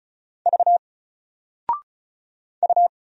Morse code, heard as a keyed tone, about 35 wpm, 700 Hz. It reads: V U
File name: Single-Letter-Number-35wpm